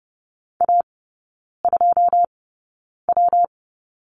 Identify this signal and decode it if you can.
Morse code, heard as a keyed tone, about 30 wpm, 700 Hz. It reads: A 2 W
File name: Lesson 19e - Letter w - Three Characters 30wpm